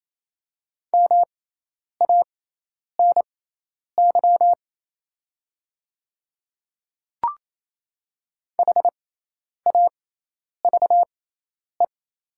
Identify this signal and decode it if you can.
Morse code, heard as a keyed tone, about 28 wpm, spaced as for 10 wpm, 700 Hz. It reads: MANY HAVE
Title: 6x Character Spacing - Top 100 Words - 28wpm